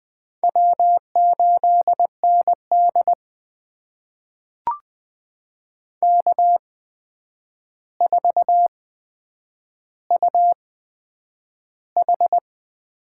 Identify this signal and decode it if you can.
Morse code, heard as a keyed tone, about 20 wpm, spaced as for 9 wpm, 700 Hz. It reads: W8ND K4UH
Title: USA Call Signs - 4 Characters - 8x Character Spacing - 20wpm